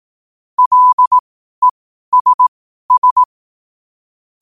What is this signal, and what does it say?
Morse code, heard as a keyed tone, about 18 wpm, spaced as for 12 wpm, 1000 Hz. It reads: LESS
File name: qotdfiles / 12wpm